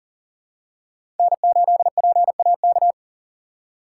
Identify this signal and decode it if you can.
Morse code, heard as a keyed tone, about 40 wpm, 700 Hz. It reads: N8PAK
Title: USA Call Signs - 5 Characters - 3x Character Spacing - 40wpm